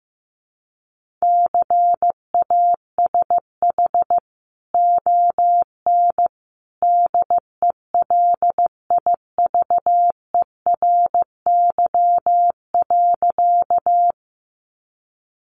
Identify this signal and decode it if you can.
Morse code, heard as a keyed tone, about 15 wpm, 700 Hz. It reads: CASH ON DELIVERY.
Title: Sentences from Top 1500 Words 15wpm